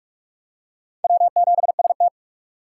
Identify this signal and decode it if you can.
Morse code, heard as a keyed tone, about 41 wpm, 700 Hz. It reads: W7ST